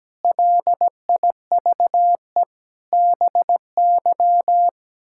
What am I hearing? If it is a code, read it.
Morse code, heard as a keyed tone, about 17 wpm, 700 Hz. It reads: LIVE BY
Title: Sentences from Top 1600 Words 17wpm